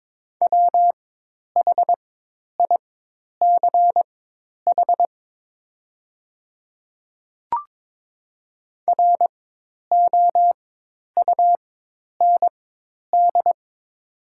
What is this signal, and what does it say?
Morse code, heard as a keyed tone, about 22 wpm, spaced as for 10 wpm, 700 Hz. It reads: WHICH ROUND
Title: 4x Character Spacing - Top 300 Words - 22wpm